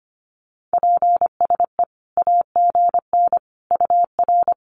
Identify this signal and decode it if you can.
Morse code, heard as a keyed tone, about 25 wpm, 700 Hz. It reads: PSE AGN UR
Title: Common QSO Phrases 25wpm